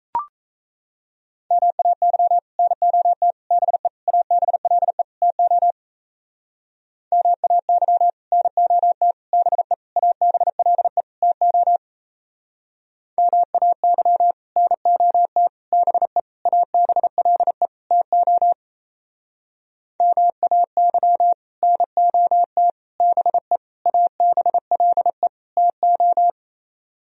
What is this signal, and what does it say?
Morse code, heard as a keyed tone, about 35 wpm, 700 Hz. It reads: MAY NOT BE ABLE TO MAY NOT BE ABLE TO MAY NOT BE ABLE TO MAY NOT BE ABLE TO
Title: Speed-Racing - Sets of 5 Words 28wpm